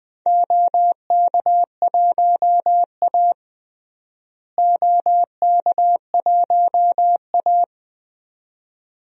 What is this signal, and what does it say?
Morse code, heard as a keyed tone, about 20 wpm, 700 Hz. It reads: OK1A OK1A
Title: DXCC-100 Call Signs - 4 Characters - Repeated 2x - 20wpm